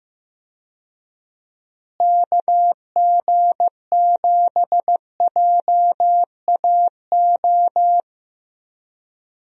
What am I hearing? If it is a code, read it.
Morse code, heard as a keyed tone, about 15 wpm, 700 Hz. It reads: KG7JAO
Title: USA Call Signs - 6 Characters - Repeated 3x - 15wpm